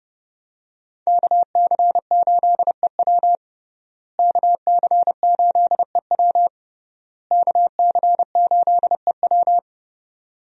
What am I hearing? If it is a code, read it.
Morse code, heard as a keyed tone, about 30 wpm, 700 Hz. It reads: KC8EW KC8EW KC8EW
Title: USA Call Signs - 5 Characters - Repeated 3x - 30wpm